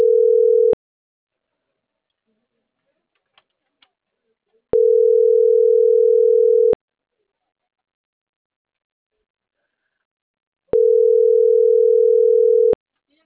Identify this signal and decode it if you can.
Ringback tone (North America), 440 + 480 Hz.